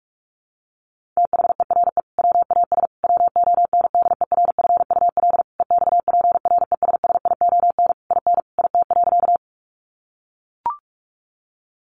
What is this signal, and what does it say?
Morse code, heard as a keyed tone, about 44 wpm, 700 Hz. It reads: THERE WAS WONDERFUL EXPRESSION IN IT.
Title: Sentences from Top 1200 Words 45wpm